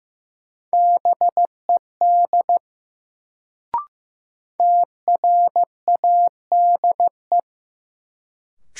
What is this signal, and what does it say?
Morse code, heard as a keyed tone, about 16 wpm, 700 Hz. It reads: BED TRADE
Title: Sets of 5 Words Encore - New Word Drill - 15wpm